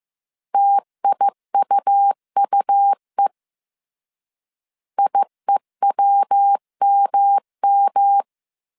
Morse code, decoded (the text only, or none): TIUUE IEWMM